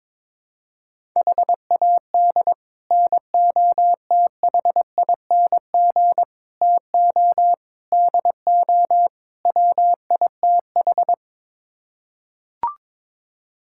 Morse code, heard as a keyed tone, about 22 wpm, 700 Hz. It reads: HAD NOTHING TO DO WITH
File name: practice / 22wpm / Build A Phrase - 5-Word Phrases - 22wpm